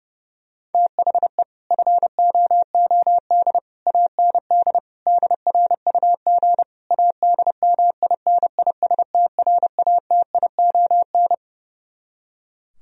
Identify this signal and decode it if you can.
Morse code, heard as a keyed tone, about 28 wpm, 700 Hz. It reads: THE FOOD AND DRUG ADMINISTRATION
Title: Speed-Racing - Sets of 5 Words 30wpm